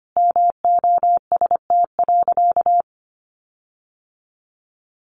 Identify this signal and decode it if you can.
Morse code, heard as a keyed tone, about 25 wpm, 700 Hz. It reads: MOST.
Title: Sentences from Top 1400 Words 25wpm